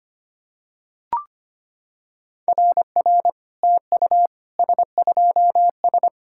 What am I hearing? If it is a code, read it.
Morse code, heard as a keyed tone, about 25 wpm, 690 Hz. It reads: RR TU S2S
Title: SOTA QSO Practice 25wpm